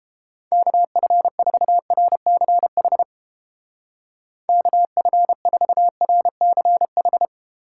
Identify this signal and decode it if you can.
Morse code, heard as a keyed tone, about 31 wpm, 700 Hz. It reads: KF4RCH KF4RCH